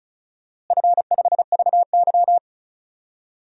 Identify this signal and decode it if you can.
Morse code, heard as a keyed tone, about 35 wpm, 700 Hz. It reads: F5VY
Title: DXCC-100 Call Signs - 4 Characters - Repeated 3x - 35wpm